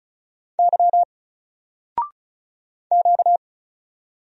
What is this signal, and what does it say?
Morse code, heard as a keyed tone, about 36 wpm, 700 Hz. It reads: Y Q
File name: Single-Letter-Number-35wpm